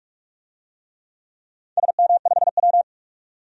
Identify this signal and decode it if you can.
Morse code, heard as a keyed tone, about 47 wpm, 700 Hz. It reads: SM5W
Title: DXCC-100 Call Signs - 4 Characters - 45wpm